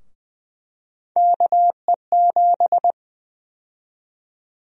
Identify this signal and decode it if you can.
Morse code, heard as a keyed tone, about 20 wpm, 700 Hz. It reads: KE7
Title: USA Call Sign Prefixes - Repeated 2x - 20wpm